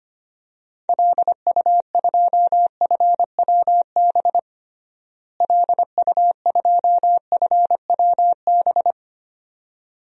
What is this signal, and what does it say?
Morse code, heard as a keyed tone, about 25 wpm, 700 Hz. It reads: LU2FWB LU2FWB
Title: DXCC-100 Call Signs - 6 Characters - Repeated 2x - 25wpm